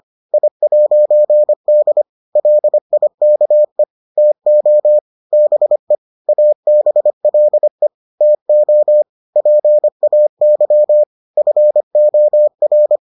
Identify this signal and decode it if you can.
Morse code, heard as a keyed tone, about 25 wpm, 600 Hz. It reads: I'D LIKE TO BE ABLE TO PAY FOR